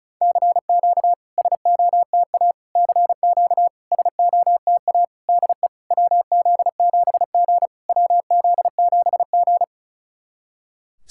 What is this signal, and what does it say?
Morse code, heard as a keyed tone, about 36 wpm, 690 Hz. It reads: CQ SOTA CQ SOTA DE WZ7G WZ7G
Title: SOTA QSO Practice 35wpm